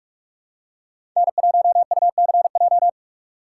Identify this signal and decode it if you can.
Morse code, heard as a keyed tone, about 46 wpm, 700 Hz. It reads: N1UCJ
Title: USA Call Signs - 5 Characters - 2x Character Spacing - 45wpm